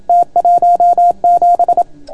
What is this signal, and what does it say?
Morse code, heard as a keyed tone, about 27 wpm, 690 Hz. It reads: T17